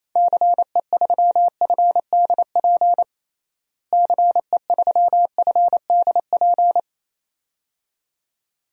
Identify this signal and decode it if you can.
Morse code, heard as a keyed tone, about 27 wpm, 700 Hz. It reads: CE3FDP CE3FDP
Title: DXCC-100 Call Signs - 6 Characters - Repeated 2x - 28wpm